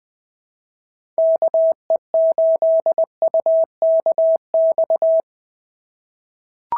Morse code, heard as a keyed tone, about 19 wpm, 690 Hz. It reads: KE8UKX
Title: POTA QSO Practice - 20wpm